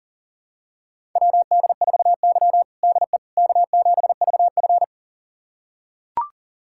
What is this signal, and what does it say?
Morse code, heard as a keyed tone, about 40 wpm, 700 Hz. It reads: WD4Y DE K7VF
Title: USA Callsign DE Callsign - 4 Characters - 40wpm